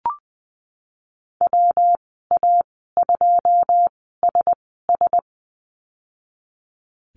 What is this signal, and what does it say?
Morse code, heard as a keyed tone, about 20 wpm, spaced as for 15 wpm, 700 Hz. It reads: WA2SS
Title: 140 Lesson 23g - Letter f - Call Signs 20wpm